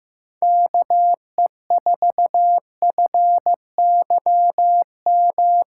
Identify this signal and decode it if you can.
Morse code, heard as a keyed tone, about 14 wpm, 700 Hz. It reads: KE4FYM
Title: USA Callsign DE Callsign - 6 Characters - 15wpm